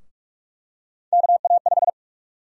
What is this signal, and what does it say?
Morse code, heard as a keyed tone, about 46 wpm, 700 Hz. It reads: KA5
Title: USA Call Sign Prefixes - 45wpm